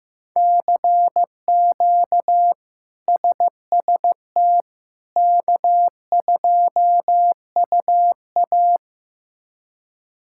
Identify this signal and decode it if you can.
Morse code, heard as a keyed tone, about 15 wpm, 700 Hz. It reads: CQ SST K2UA
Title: K1USN SST QSO Practice - 15wpm